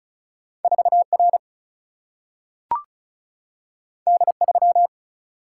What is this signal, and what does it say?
Morse code, heard as a keyed tone, about 36 wpm, 700 Hz. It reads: 4R D3
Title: Single-Letter Single-Number 35wpm